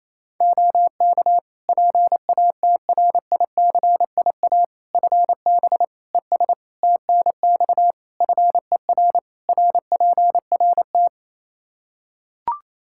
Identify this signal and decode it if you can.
Morse code, heard as a keyed tone, about 28 wpm, 700 Hz. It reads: OK PATRICIA FB ES TNX FER RPRT
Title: Practice QSO - Exchange 2 - Info - 28wpm